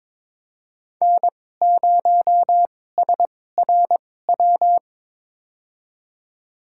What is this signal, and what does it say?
Morse code, heard as a keyed tone, about 22 wpm, spaced as for 16 wpm, 700 Hz. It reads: N0SRW
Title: USA Call Signs - 5 Characters - 2x Character Spacing - 22wpm